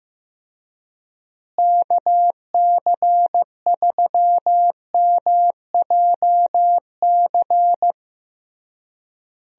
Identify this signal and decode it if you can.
Morse code, heard as a keyed tone, about 14 wpm, 700 Hz. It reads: KC3MJC
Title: USA Call Signs - 6 Characters - Repeated 3x - 15wpm